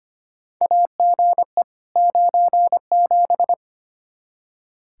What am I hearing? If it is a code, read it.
Morse code, heard as a keyed tone, about 25 wpm, 700 Hz. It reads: AGE 97
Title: Practice QSO - Exchange 3 - More Info - 25wpm